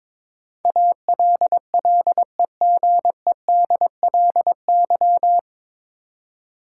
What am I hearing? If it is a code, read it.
Morse code, heard as a keyed tone, about 22 wpm, 700 Hz. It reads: ALLEGEDLY
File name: 9 Letter Words - First 3 Letters Given - 22wpm